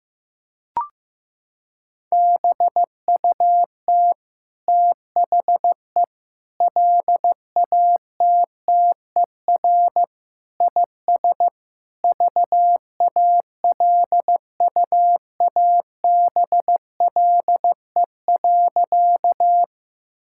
Morse code, heard as a keyed tone, about 15 wpm, 700 Hz. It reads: BUT THE LATTER IS VALUABLE.
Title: Sentences from Top 1600 Words 15wpm